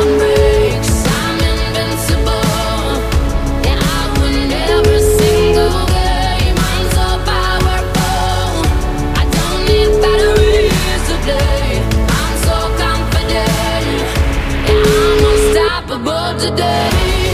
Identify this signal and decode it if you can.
Ringback tone (Europe, 425 Hz), with noise.